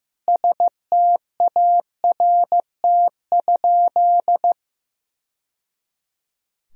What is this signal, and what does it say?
Morse code, heard as a keyed tone, about 16 wpm, 700 Hz. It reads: START?
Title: Sentences from Top 1000 Words - Review Part 1 - 15wpm